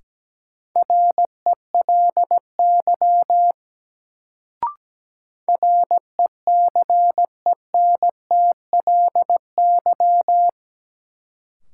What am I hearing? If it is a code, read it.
Morse code, heard as a keyed tone, about 17 wpm, 700 Hz. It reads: RELY RECENTLY
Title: Sentences from Top 1200 Words - New Word Drill - 17wpm